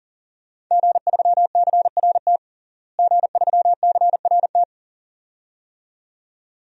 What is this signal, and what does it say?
Morse code, heard as a keyed tone, about 40 wpm, 700 Hz. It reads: G3CRT G3CRT
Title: DXCC-100 Call Signs - 5 Characters - Repeated 2x - 40wpm